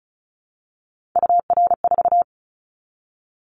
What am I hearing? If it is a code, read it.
Morse code, heard as a keyed tone, about 36 wpm, 700 Hz. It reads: UR4
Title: Callsign Prefixes for Top 25 DXCC - 35wpm